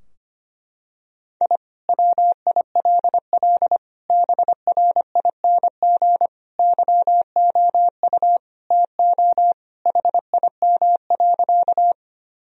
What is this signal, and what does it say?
Morse code, heard as a keyed tone, about 25 wpm, 700 Hz. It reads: I WILL BRING YOU TO HIM.
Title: Sentences from Top 1000 Words - Review Part 3 - 25wpm